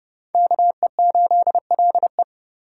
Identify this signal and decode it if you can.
Morse code, heard as a keyed tone, about 27 wpm, 700 Hz. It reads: KE8LE